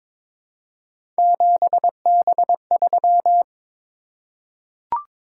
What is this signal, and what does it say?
Morse code, heard as a keyed tone, about 22 wpm, 700 Hz. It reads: 7B3
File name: practice / 22wpm / Three Letter-Number Combinations - 22wpm